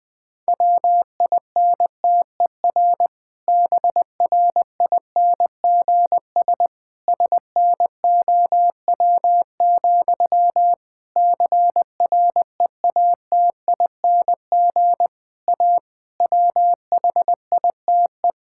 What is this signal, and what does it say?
Morse code, heard as a keyed tone, about 20 wpm, 700 Hz. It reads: WINTER BRINGS SNOW, CREATING A WHITE
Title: Difficult Sentences with Hints - 20wpm